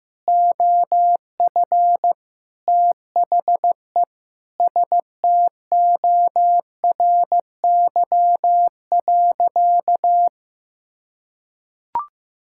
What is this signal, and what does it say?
Morse code, heard as a keyed tone, about 15 wpm, 700 Hz. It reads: OF THE STORY.
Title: Sentences from Top 1000 Words - Review Part 5 - 15wpm